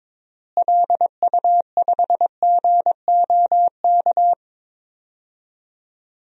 Morse code, heard as a keyed tone, about 22 wpm, 700 Hz. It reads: LU5GOK
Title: DXCC-100 Call Signs - 6 Characters - Repeated 3x - 22wpm